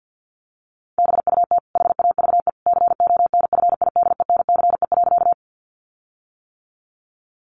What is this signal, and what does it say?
Morse code, heard as a keyed tone, about 50 wpm, 700 Hz. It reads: BUT HAVE CONFIDENCE.
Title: Sentences from Top 1500 Words 50wpm